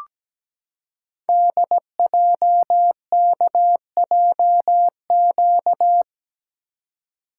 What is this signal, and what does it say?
Morse code, heard as a keyed tone, about 17 wpm, 700 Hz. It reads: DJKJQ
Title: Code Groups - 5 Letters - 17wpm